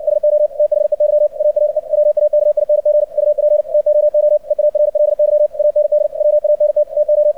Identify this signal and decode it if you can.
Morse code, heard as a keyed tone, about 60 wpm, 600 Hz, with radio noise. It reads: SO THEY ARE WAKING UP TOO EARLY AND WANT TO
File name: archived / CRSnet